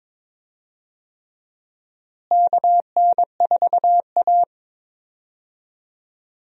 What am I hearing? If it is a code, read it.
Morse code, heard as a keyed tone, about 22 wpm, 700 Hz. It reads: KN4A
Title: USA Call Signs - 4 Characters - Repeated 3x - 22wpm